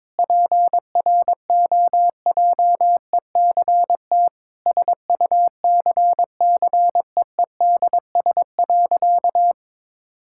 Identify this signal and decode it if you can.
Morse code, heard as a keyed tone, about 21 wpm, 700 Hz. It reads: PROJECT SUCCEEDS.
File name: practice / 22wpm / Sentences with 8-Word Phrases - 22wpm